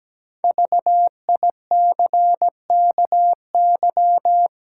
Morse code, heard as a keyed tone, about 17 wpm, 700 Hz. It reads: VICKY
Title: Pangram Challenge v2 - 17wpm